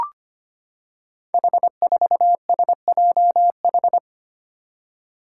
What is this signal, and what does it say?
Morse code, heard as a keyed tone, about 25 wpm, 700 Hz. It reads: H4SJH
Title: Five Challenging Letter-Number Combinations 25wpm